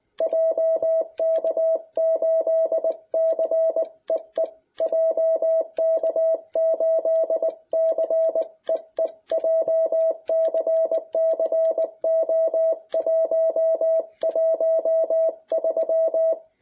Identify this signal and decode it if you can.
Morse code, heard as a keyed tone, about 20 wpm, 650 Hz. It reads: JK8CEE JK8CEE JCC O113